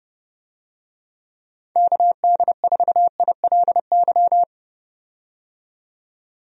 Morse code, heard as a keyed tone, about 29 wpm, 700 Hz. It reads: KD4ILY
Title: USA Call Signs - 6 Characters - 30wpm